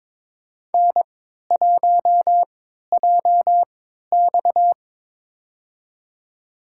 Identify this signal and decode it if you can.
Morse code, heard as a keyed tone, about 22 wpm, spaced as for 12 wpm, 700 Hz. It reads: N1JX
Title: USA Call Signs - 4 Characters - 3x Character Spacing - 22wpm